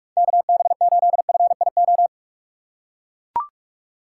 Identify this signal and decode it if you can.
Morse code, heard as a keyed tone, about 42 wpm, 700 Hz. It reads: KB8FIO
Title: USA Call Signs - 6 Characters - 6x Character Spacing - 45wpm